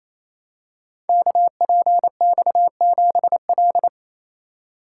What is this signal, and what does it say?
Morse code, heard as a keyed tone, about 27 wpm, 700 Hz. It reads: KPX7L
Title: Code Groups - 5 Letter:Numbers - 28wpm